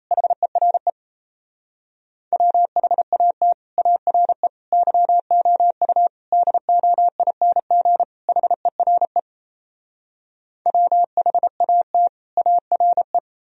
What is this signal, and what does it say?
Morse code, heard as a keyed tone, about 37 wpm, 700 Hz. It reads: HERE WHAT ARE YOU DOING HERE WHAT ARE